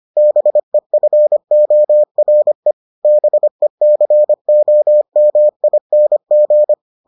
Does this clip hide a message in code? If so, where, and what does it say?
Yes: Morse code from 0.2 to 6.7 s, heard as a keyed tone, about 25 wpm, 600 Hz. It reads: BEFORE BECOMING